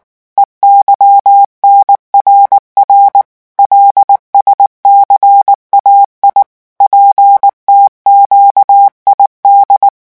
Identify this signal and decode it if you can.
Morse code, heard as a keyed tone, about 19 wpm, 800 Hz. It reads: EYNRR LSCAI PTQID
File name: morse mp3 / revision7-vitesse 18 mots